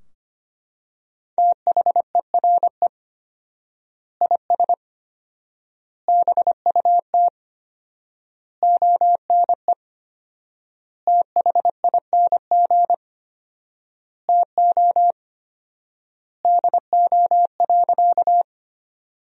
Morse code, heard as a keyed tone, about 25 wpm, 700 Hz. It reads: THERE IS BUT ONE THING TO DO.
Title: 4x Word-Spacing - Sentences from Top 100 Words - 25wpm